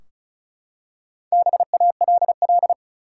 Morse code, heard as a keyed tone, about 36 wpm, 700 Hz. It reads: BALL
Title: Four Letter Words - 35wpm